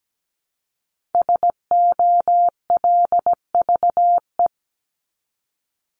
Morse code, heard as a keyed tone, about 17 wpm, 700 Hz. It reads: SOLVE
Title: Top 751-1000 Words 17wpm